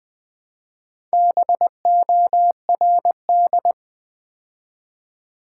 Morse code, heard as a keyed tone, about 20 wpm, 700 Hz. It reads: BORD
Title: Build A Word - 2 Letters At A Time - 6-Letter Words - 20wpm